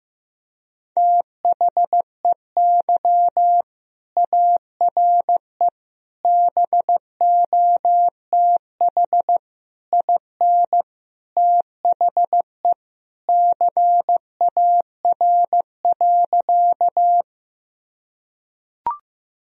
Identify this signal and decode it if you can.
Morse code, heard as a keyed tone, about 15 wpm, 700 Hz. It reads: THEY ARE BOTH IN THE CAR.
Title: Sentences from Top 1000 Words - Review Part 1 - 15wpm